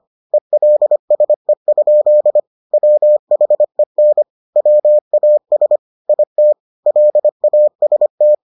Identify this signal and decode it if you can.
Morse code, heard as a keyed tone, about 25 wpm, 600 Hz. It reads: ELSE? WHEN WAS IT LAST